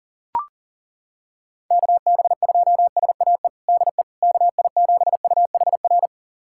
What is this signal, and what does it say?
Morse code, heard as a keyed tone, about 40 wpm, 700 Hz. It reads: KB2SAE DE KI7UHR
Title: USA Callsign DE Callsign - 6 Characters - 40wpm